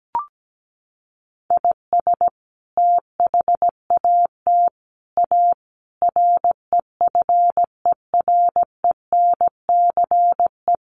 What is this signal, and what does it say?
Morse code, heard as a keyed tone, about 17 wpm, 700 Hz. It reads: IS THAT A REFERENCE
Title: Sentences from Top 1200 Words 17wpm